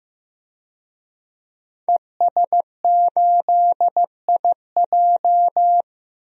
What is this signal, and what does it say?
Morse code, heard as a keyed tone, about 16 wpm, 700 Hz. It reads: ES8IJ
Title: DXCC-100 Call Signs - 5 Characters - Repeated 2x - 15wpm